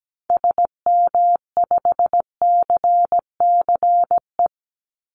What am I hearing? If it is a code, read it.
Morse code, heard as a keyed tone, about 17 wpm, 700 Hz. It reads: SM5CCE
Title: Call Signs Repeated 3x - 17wpm